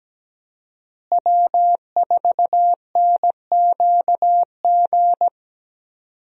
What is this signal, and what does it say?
Morse code, heard as a keyed tone, about 17 wpm, 700 Hz. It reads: W4NQG
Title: USA Call Signs - 5 Characters - Repeated 2x - 17wpm